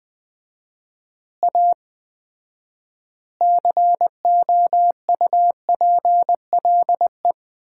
Morse code, heard as a keyed tone, about 21 wpm, 700 Hz. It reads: A COUPLE
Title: 4x Word-Spacing - Sets of 4 Words - 20wpm